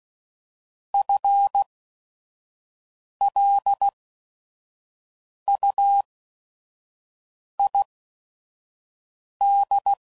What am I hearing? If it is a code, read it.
Morse code, heard as a keyed tone, about 16 wpm, spaced as for 5 wpm, 800 Hz. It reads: FLUID